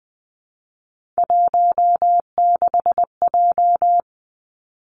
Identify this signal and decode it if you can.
Morse code, heard as a keyed tone, about 20 wpm, 700 Hz. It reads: 16J
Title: Three Challenging Letter-Number Combinations 20wpm